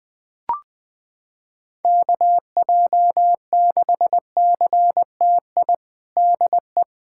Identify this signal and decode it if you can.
Morse code, heard as a keyed tone, about 20 wpm, 700 Hz. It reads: KJ6CTI DE
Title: USA Callsign DE Callsign - 6 Characters - 20wpm